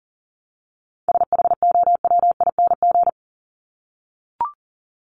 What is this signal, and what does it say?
Morse code, heard as a keyed tone, about 40 wpm, 700 Hz. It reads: SHOWING
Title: ING Suffix - Top 100 Words with Base Words 40wpm